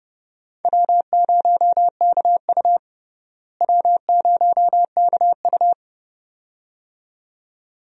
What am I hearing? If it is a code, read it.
Morse code, heard as a keyed tone, about 30 wpm, 700 Hz. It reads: W0KU W0KU
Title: USA Call Signs - 4 Characters - Repeated 2x - 30wpm